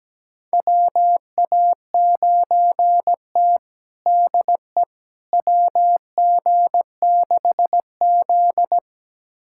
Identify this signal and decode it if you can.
Morse code, heard as a keyed tone, about 17 wpm, 700 Hz. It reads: WA9T DE WG6Z
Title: USA Callsign DE Callsign - 4 Characters - 17wpm